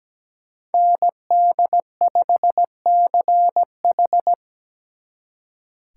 Morse code, heard as a keyed tone, about 20 wpm, 700 Hz. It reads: ND5CH
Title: Speed Racing - USA 2x2 Call Signs 17wpm